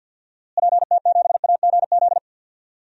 Morse code, heard as a keyed tone, about 52 wpm, 700 Hz. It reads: PT7AGZ
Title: DXCC-100 Call Signs - 6 Characters - Repeated 3x - 50wpm